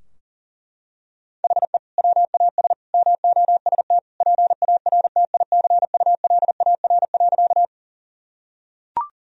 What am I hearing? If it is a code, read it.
Morse code, heard as a keyed tone, about 40 wpm, 700 Hz. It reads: HE WAS MOST PARTICULAR.
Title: Sentences from Top 1000 Words - Review Part 4 - 40wpm